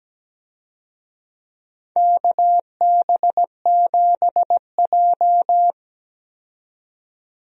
Morse code, heard as a keyed tone, about 17 wpm, 700 Hz. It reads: KB7J
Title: USA Call Signs - 4 Characters - 17wpm